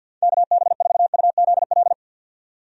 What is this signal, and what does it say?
Morse code, heard as a keyed tone, about 46 wpm, 700 Hz. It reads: KB4UZL
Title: USA Call Signs - 6 Characters - 50wpm